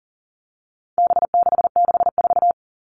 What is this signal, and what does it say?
Morse code, heard as a keyed tone, about 40 wpm, 700 Hz. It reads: B664
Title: All Cut Numbers - 4-Digit 40wpm